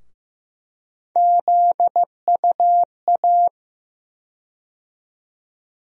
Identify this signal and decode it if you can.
Morse code, heard as a keyed tone, about 15 wpm, 700 Hz. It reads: ZUA
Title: USA Call Sign Postfixes - 3 Letters 15wpm